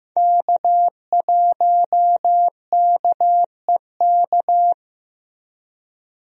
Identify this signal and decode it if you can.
Morse code, heard as a keyed tone, about 15 wpm, 700 Hz. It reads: K1KEK
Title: USA Call Signs - 5 Characters - 15wpm